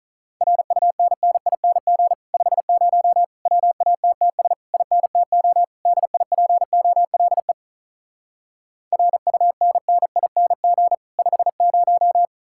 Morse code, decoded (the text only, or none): RUNNING 50 WATTS INTO DIPOLE RUNNING 50